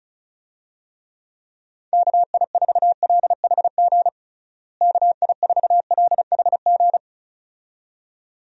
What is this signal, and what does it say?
Morse code, heard as a keyed tone, about 34 wpm, 700 Hz. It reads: KI4LHG KI4LHG
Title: USA Call Signs - 6 Characters - Repeated 2x - 35wpm